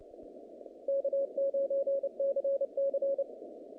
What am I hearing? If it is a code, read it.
Morse code, heard as a keyed tone, about 29 wpm, 550 Hz. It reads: K9CC
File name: contests / naqp-cw-k5tr-op-k5pi- -2200